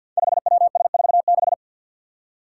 Morse code, heard as a keyed tone, about 51 wpm, 700 Hz. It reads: 5WS46
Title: Five Challenging Letter-Number Combinations 50wpm